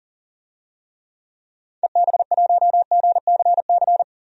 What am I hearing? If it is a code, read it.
Morse code, heard as a keyed tone, about 38 wpm, 700 Hz. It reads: EB1GCC
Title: DXCC-100 Call Signs - 6 Characters - 40wpm